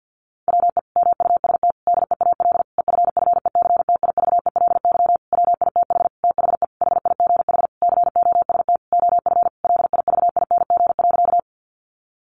Morse code, heard as a keyed tone, about 50 wpm, 700 Hz. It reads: WE MUST DEAL EFFECTIVELY WITH THE HIGH COST OF LIVING.
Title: Sentences from Top 1400 Words 50wpm